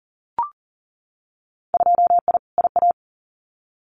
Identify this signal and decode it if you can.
Morse code, heard as a keyed tone, about 38 wpm, 700 Hz. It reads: 2I IA
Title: Winter Field Day Exchanges Encore 40wpm